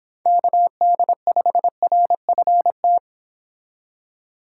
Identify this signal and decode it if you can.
Morse code, heard as a keyed tone, about 30 wpm, 700 Hz. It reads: KD5RFT